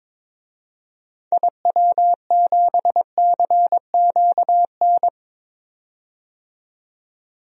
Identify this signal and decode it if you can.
Morse code, heard as a keyed tone, about 22 wpm, 700 Hz. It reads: IW7CQN
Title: DXCC-100 Call Signs - 6 Characters - 22wpm